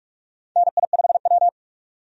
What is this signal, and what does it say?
Morse code, heard as a keyed tone, about 46 wpm, 700 Hz. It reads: NI5W
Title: USA Call Signs - 4 Characters - 45wpm